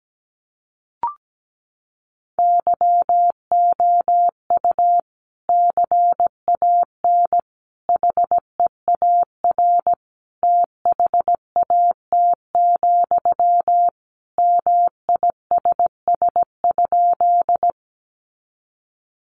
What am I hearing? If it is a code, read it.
Morse code, heard as a keyed tone, about 17 wpm, 700 Hz. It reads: YOU CAN HEAR THAT, MISS?
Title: Sentences from Top 500 Words 17wpm